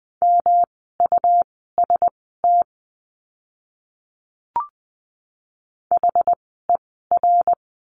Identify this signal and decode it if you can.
Morse code, heard as a keyed tone, about 20 wpm, spaced as for 15 wpm, 700 Hz. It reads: MUST HER
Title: 237 Lesson 38f - Letter q - Words 20wpm